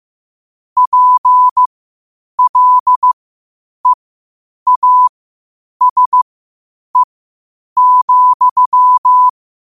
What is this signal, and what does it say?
Morse code, heard as a keyed tone, about 15 wpm, spaced as for 8 wpm, 1000 Hz. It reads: PLEASE,
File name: qotdfiles / 8wpm